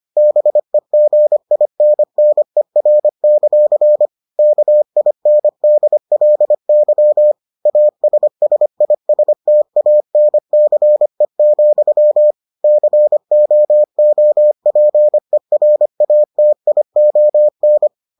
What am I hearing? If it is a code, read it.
Morse code, heard as a keyed tone, about 25 wpm, 600 Hz. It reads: BEGINNER; KINDLY ASSISTANCE, COOPERATION